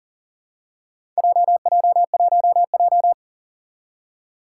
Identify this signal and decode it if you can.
Morse code, heard as a keyed tone, about 40 wpm, 700 Hz. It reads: JJ1J
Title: Four Challenging Letter-Number Combinations 40wpm